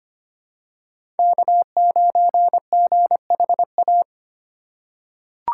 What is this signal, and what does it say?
Morse code, heard as a keyed tone, about 25 wpm, 700 Hz. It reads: K9GHA
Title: USA Call Signs - 3x Character Spacing - 25wpm